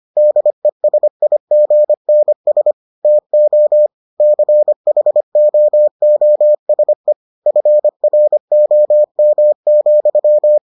Morse code, heard as a keyed tone, about 25 wpm, 600 Hz. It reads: DESIGNS TO CHOOSE FROM,